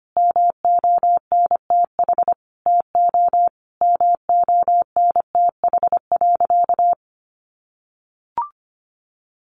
Morse code, heard as a keyed tone, about 25 wpm, 700 Hz. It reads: MONTH TO MONTH.
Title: Sentences from Top 1200 Words 25wpm